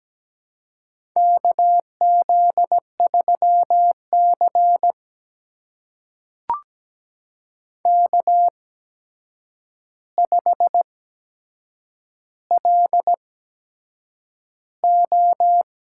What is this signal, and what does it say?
Morse code, heard as a keyed tone, about 17 wpm, spaced as for 7 wpm, 700 Hz. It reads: KZ3C K5LO
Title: USA Call Signs - 4 Characters - 8x Character Spacing - 17wpm